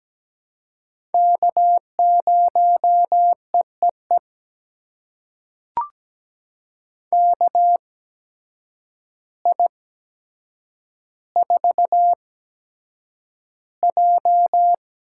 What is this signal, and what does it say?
Morse code, heard as a keyed tone, about 17 wpm, spaced as for 8 wpm, 700 Hz. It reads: K0EEE KI4J
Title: USA Call Signs - 8x Character Spacing - 17wpm